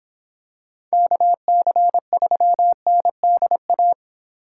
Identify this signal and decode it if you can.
Morse code, heard as a keyed tone, about 26 wpm, 700 Hz. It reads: KC3NDA